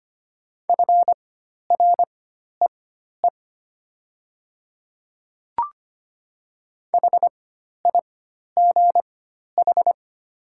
Morse code, heard as a keyed tone, about 25 wpm, spaced as for 12 wpm, 700 Hz. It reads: FREE HIGH
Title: 4x Character Spacing - Top 500 Words - 25wpm